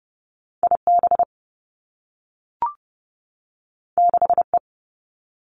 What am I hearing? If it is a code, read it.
Morse code, heard as a keyed tone, about 30 wpm, 700 Hz. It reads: I B 6 E
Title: Two Challenging Letter-Number Combinations 30wpm